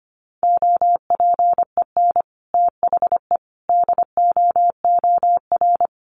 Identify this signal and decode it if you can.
Morse code, heard as a keyed tone, about 25 wpm, 700 Hz. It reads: OPEN THE DOOR
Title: Sentences from Top 1300 Words 25wpm